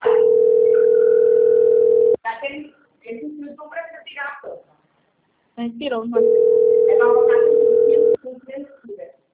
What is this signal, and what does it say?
Ringback tone (North America), 440 + 480 Hz, with speech.
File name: out- - - - - .86145